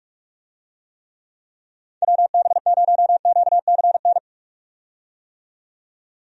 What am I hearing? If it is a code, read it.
Morse code, heard as a keyed tone, about 46 wpm, 700 Hz. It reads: WB0QCN
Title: USA Call Signs - 6 Characters - 45wpm